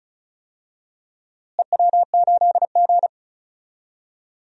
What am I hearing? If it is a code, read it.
Morse code, heard as a keyed tone, about 36 wpm, 700 Hz. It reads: EW8G
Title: DXCC-100 Call Signs - 4 Characters - Repeated 3x - 35wpm